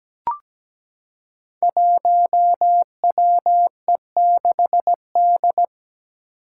Morse code, heard as a keyed tone, about 17 wpm, 700 Hz. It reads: 1WE6D
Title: Five Challenging Letter-Number Combinations 17wpm